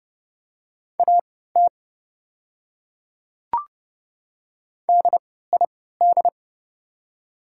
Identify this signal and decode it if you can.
Morse code, heard as a keyed tone, about 30 wpm, spaced as for 17 wpm, 690 Hz. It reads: AT DID
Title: 3x Character Spacing - Top 100 Words - 30wpm